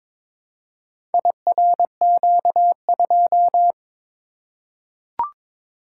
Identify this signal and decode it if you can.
Morse code, heard as a keyed tone, about 21 wpm, 690 Hz. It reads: IRQ2
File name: practice / 22wpm / Four Letter-Number Combinations - 22wpm